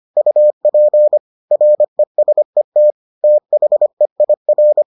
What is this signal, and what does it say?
Morse code, heard as a keyed tone, about 25 wpm, 600 Hz. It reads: UP RESET THEIR